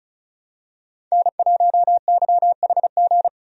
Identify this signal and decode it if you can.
Morse code, heard as a keyed tone, about 36 wpm, 700 Hz. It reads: N1YHG
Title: USA Call Signs - 5 Characters - 35wpm